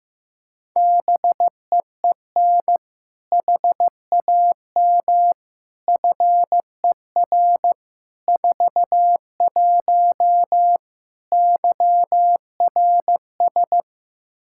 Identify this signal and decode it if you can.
Morse code, heard as a keyed tone, about 15 wpm, 700 Hz. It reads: BEEN HAM FER 41 YRS
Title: K1USN SST QSO Practice 15wpm